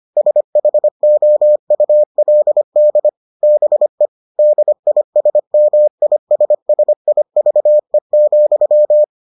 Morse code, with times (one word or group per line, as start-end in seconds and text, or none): 0.2-3.1 SHOULD
3.4-4.1 BE
4.4-9.0 DISMISSIVE,